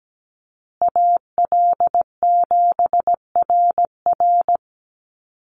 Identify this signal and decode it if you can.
Morse code, heard as a keyed tone, about 17 wpm, 700 Hz. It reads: AL7RR
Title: USA Call Signs Repeated 3x 17wpm